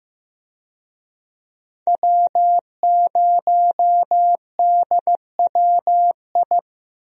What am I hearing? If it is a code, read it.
Morse code, heard as a keyed tone, about 15 wpm, 700 Hz. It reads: W0DWI
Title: USA Call Signs - 5 Characters - Repeated 2x - 15wpm